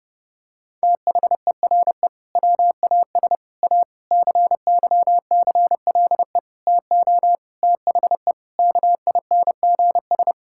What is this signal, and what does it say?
Morse code, heard as a keyed tone, about 30 wpm, 700 Hz. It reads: THERE WAS A CYCLE TO THE KINGS
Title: Sentences from Top 1000 Words - Review Part 6 - 30wpm